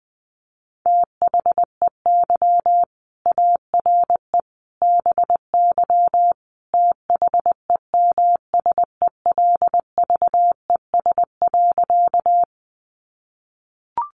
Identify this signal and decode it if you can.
Morse code, heard as a keyed tone, about 20 wpm, 700 Hz. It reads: THEY ARE BY THEMSELVES.
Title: Sentences from Top 800 Words 20wpm